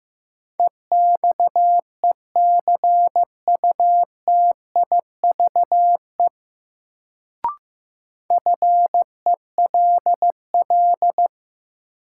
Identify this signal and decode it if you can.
Morse code, heard as a keyed tone, about 15 wpm, 700 Hz. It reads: EXECUTIVE FELL
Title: Sets of 5 Words Encore - New Word Drill - 15wpm